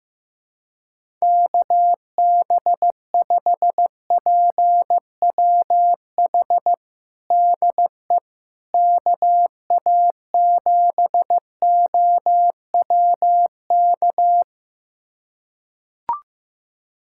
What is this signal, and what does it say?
Morse code, heard as a keyed tone, about 16 wpm, 700 Hz. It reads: KB5PWH DE KA7OWK
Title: USA Callsign DE Callsign 15wpm